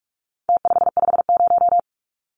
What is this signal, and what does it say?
Morse code, heard as a keyed tone, about 44 wpm, 700 Hz. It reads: T550
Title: Numbers - Contest Style - 4-Digit 45wpm